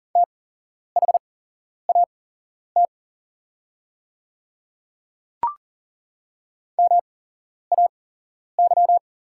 Morse code, heard as a keyed tone, about 40 wpm, spaced as for 11 wpm, 700 Hz. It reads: THAT MAY
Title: 8x Character Spacing - Top 100 Words - 40wpm